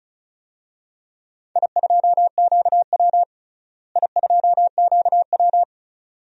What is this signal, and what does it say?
Morse code, heard as a keyed tone, about 34 wpm, 700 Hz. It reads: I2QW I2QW
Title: DXCC-100 Call Signs - 4 Characters - Repeated 3x - 35wpm